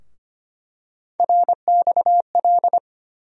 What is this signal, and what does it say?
Morse code, heard as a keyed tone, about 25 wpm, 700 Hz. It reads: RXL
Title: USA Call Sign Postfixes - 3 Letters 25wpm